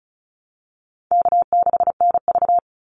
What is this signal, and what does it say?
Morse code, heard as a keyed tone, about 34 wpm, 700 Hz. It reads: K6NV
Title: Easy Call Signs Repeated 2x - 35wpm